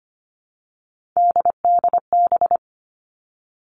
Morse code, heard as a keyed tone, about 25 wpm, 700 Hz. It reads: DDB
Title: All Cut Numbers - 3-Digit 25wpm